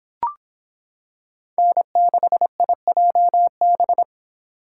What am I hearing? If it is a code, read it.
Morse code, heard as a keyed tone, about 26 wpm, 700 Hz. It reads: N6IJB